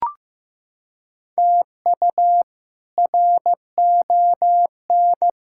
Morse code, heard as a keyed tone, about 15 wpm, 700 Hz. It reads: TU RON
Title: K1USN SST QSO Practice - 15wpm